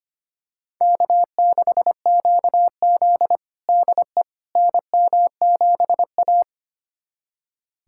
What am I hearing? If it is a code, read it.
Morse code, heard as a keyed tone, about 25 wpm, 700 Hz. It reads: K6QZ DE NM7A
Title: USA Callsign DE Callsign - 4 Characters - 25wpm